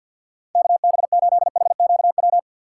Morse code, heard as a keyed tone, about 48 wpm, 700 Hz. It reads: KB8HQW
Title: USA Call Signs - 6 Characters - 4x Character Spacing - 50wpm